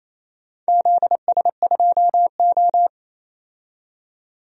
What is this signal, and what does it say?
Morse code, heard as a keyed tone, about 27 wpm, 700 Hz. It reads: ZS2O
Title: DXCC-100 Call Signs - 4 Characters - 28wpm